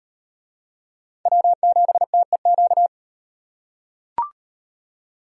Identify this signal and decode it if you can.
Morse code, heard as a keyed tone, about 38 wpm, 700 Hz. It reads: W7TEQ